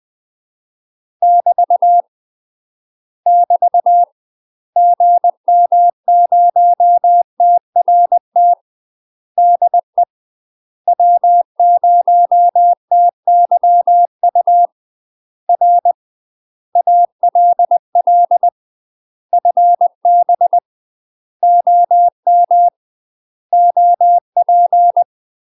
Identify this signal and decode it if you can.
Morse code, heard as a keyed tone, about 20 wpm, 700 Hz. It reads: <BT> <BT> GM0TRT DE W0TYU R ALL FB OM OP
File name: UK MASTER QSOs TRAINING / 20WPM / UK MASTER QSOs 0073